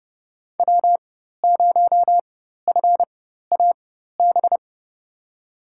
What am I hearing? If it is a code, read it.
Morse code, heard as a keyed tone, about 30 wpm, spaced as for 14 wpm, 700 Hz. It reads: W0FAB
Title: USA Call Signs - 5 Characters - 4x Character Spacing - 30wpm